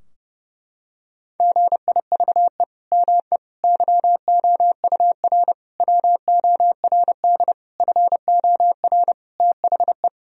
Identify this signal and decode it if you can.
Morse code, heard as a keyed tone, about 30 wpm, 700 Hz. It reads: GIVE ME YOUR WORD FOR THE
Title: Sentences from Top 1000 Words - Review Part 3 - 30wpm